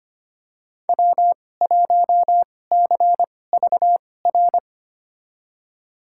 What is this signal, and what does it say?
Morse code, heard as a keyed tone, about 25 wpm, spaced as for 18 wpm, 700 Hz. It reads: W1CVR
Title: USA Call Signs - 5 Characters - 2x Character Spacing - 25wpm